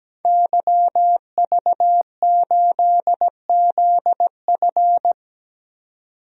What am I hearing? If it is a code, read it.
Morse code, heard as a keyed tone, about 17 wpm, 700 Hz. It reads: YV8ZF
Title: DXCC-100 Call Signs - 5 Characters - 17wpm